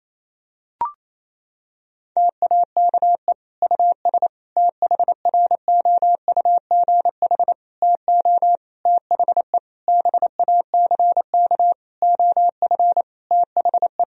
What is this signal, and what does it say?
Morse code, heard as a keyed tone, about 27 wpm, 700 Hz. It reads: TAKE US THROUGH TO THE BACK OF THE
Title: Sentences from Top 1000 Words - Review Part 1 - 28wpm